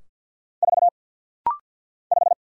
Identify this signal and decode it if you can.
Morse code, heard as a keyed tone, about 50 wpm, 700 Hz. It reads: 4 5
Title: Single Letter-Number Alt - Rapid-Fire 50wpm